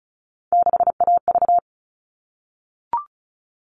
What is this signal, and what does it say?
Morse code, heard as a keyed tone, about 34 wpm, 700 Hz. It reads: 6AV
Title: All Cut Numbers - 3-Digit 35wpm